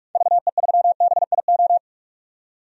Morse code, heard as a keyed tone, about 47 wpm, 700 Hz. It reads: VE3BIO
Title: DXCC-100 Call Signs - 6 Characters - 45wpm